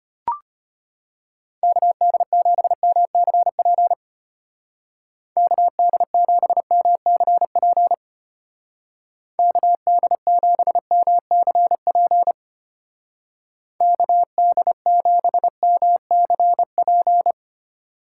Morse code, heard as a keyed tone, about 30 wpm, 700 Hz. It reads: KD7MCP KD7MCP KD7MCP KD7MCP
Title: Speed Racing - USA 2x3 Call Signs 25wpm